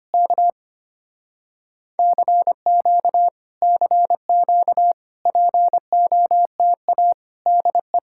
Morse code, heard as a keyed tone, about 25 wpm, 700 Hz. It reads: K CQ CQ POTA DE
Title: Speed-Racing Common QSO Phrases Encore - 25wpm